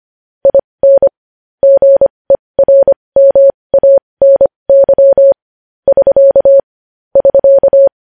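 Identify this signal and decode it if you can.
Morse code, heard as a keyed tone, about 25 wpm, spaced as for 20 wpm, 550 Hz. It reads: IN GERMANY <SK> <SK>
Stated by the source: Morse code news headlines sent at 20 wpm.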